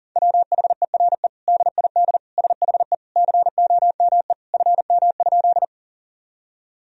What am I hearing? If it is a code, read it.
Morse code, heard as a keyed tone, about 40 wpm, 700 Hz. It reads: WHERE DID SHE COME FM?
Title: Sentences from Top 200 Words and Abbreviations 40wpm